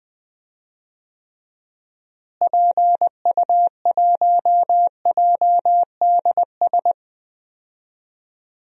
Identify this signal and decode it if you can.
Morse code, heard as a keyed tone, about 20 wpm, 700 Hz. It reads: PU1JDS
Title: DXCC-100 Call Signs - 6 Characters - Repeated 2x - 20wpm